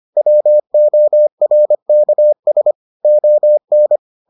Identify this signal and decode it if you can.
Morse code, heard as a keyed tone, about 25 wpm, 600 Hz. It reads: WORKS ON